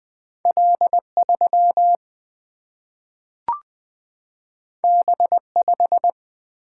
Morse code, heard as a keyed tone, about 20 wpm, 700 Hz. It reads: L3 B5
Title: Single-Letter Single-Number 20wpm